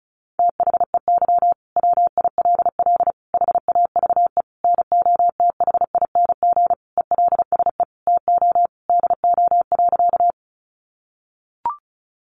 Morse code, heard as a keyed tone, about 34 wpm, 700 Hz. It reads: THEY WILL HAVE NOTHING ELSE TO DO.
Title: Sentences from Top 800 Words 35wpm